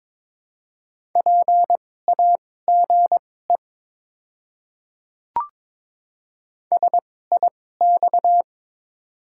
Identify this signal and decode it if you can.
Morse code, heard as a keyed tone, about 21 wpm, spaced as for 16 wpm, 700 Hz. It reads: PAGE SIX
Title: 2x Character Spacing - Top 500 Words - 22wpm